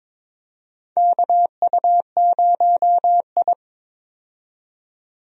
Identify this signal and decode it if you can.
Morse code, heard as a keyed tone, about 22 wpm, 700 Hz. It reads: KU0I
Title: USA Call Signs - 4 Characters - 22wpm